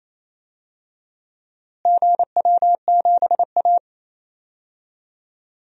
Morse code, heard as a keyed tone, about 27 wpm, 700 Hz. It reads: GW7A
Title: DXCC-100 Call Signs - 4 Characters - Repeated 2x - 28wpm